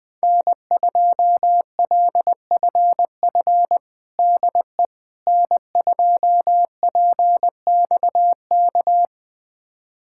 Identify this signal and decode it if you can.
Morse code, heard as a keyed tone, about 21 wpm, 700 Hz. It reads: N2LFF DE N2PXK
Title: USA Callsign DE Callsign - 5 Characters - 20wpm